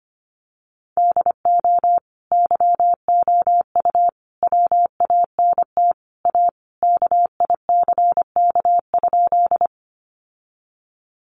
Morse code, heard as a keyed tone, about 25 wpm, 700 Hz. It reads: DO YOU WANT A KICK?
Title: Sentences from Top 1300 Words 25wpm